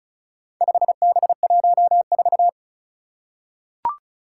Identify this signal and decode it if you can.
Morse code, heard as a keyed tone, about 36 wpm, 700 Hz. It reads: 5B14
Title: Four Challenging Letter-Number Combinations 35wpm